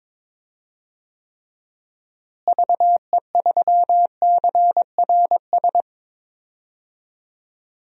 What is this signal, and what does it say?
Morse code, heard as a keyed tone, about 22 wpm, 700 Hz. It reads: VE3CRS
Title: DXCC-100 Call Signs - 6 Characters - Repeated 2x - 22wpm